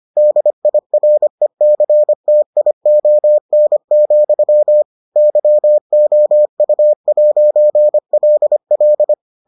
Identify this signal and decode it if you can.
Morse code, heard as a keyed tone, about 25 wpm, 600 Hz. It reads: DIRECTION, YOU'LL